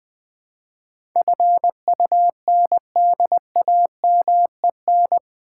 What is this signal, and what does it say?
Morse code, heard as a keyed tone, about 20 wpm, 700 Hz. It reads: FUNDAMEN
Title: Build A Word - 4 Letters At A Time - 11-Letter Words - 20wpm